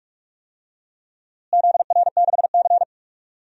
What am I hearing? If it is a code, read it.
Morse code, heard as a keyed tone, about 45 wpm, 700 Hz. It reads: ZR6C
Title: DXCC-100 Call Signs - 4 Characters - Repeated 3x - 45wpm